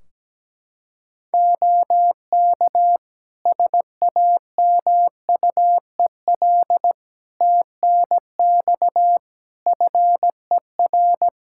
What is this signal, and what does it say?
Morse code, heard as a keyed tone, about 17 wpm, 700 Hz. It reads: OK SAMUEL TNX FER
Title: K1USN SST QSO Practice 17wpm